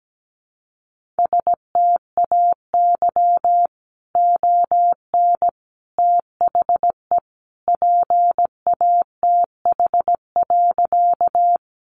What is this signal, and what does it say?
Morse code, heard as a keyed tone, about 17 wpm, 700 Hz. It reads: STAY ON THE PATH.
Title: A Sound Of Thunder 17wpm